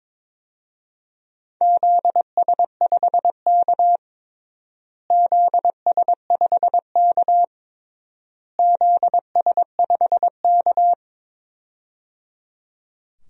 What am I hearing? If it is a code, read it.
Morse code, heard as a keyed tone, about 22 wpm, 700 Hz. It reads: ZS5K ZS5K ZS5K
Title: DXCC-100 Call Signs - 4 Characters - Repeated 3x - 22wpm